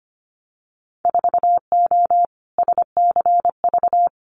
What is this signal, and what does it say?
Morse code, heard as a keyed tone, about 25 wpm, 700 Hz. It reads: 4O SCV
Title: Winter Field Day Exchanges 25wpm